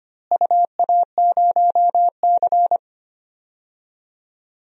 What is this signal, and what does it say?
Morse code, heard as a keyed tone, about 25 wpm, 700 Hz. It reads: UA0C
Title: DXCC-100 Call Signs - 4 Characters - Repeated 2x - 25wpm